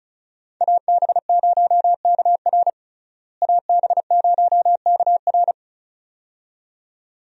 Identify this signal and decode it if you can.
Morse code, heard as a keyed tone, about 36 wpm, 700 Hz. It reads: AB0KR AB0KR
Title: USA Call Signs - 5 Characters - Repeated 2x - 35wpm